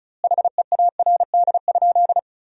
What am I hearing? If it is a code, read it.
Morse code, heard as a keyed tone, about 36 wpm, 700 Hz. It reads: HEARD?
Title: 2x Word-Spacing - Sentences from Top 400 Words - 35wpm